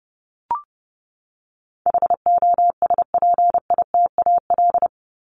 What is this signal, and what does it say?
Morse code, heard as a keyed tone, about 30 wpm, 700 Hz. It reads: HOSPITAL
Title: Top 1751-2000 Words 30wpm